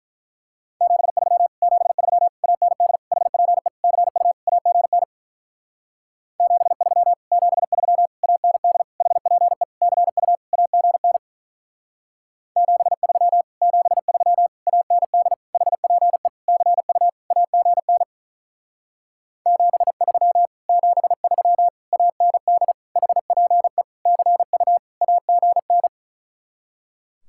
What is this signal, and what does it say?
Morse code, heard as a keyed tone, about 44 wpm, 700 Hz. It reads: 73 73 AND HPE CU AGN 73 73 AND HPE CU AGN 73 73 AND HPE CU AGN 73 73 AND HPE CU AGN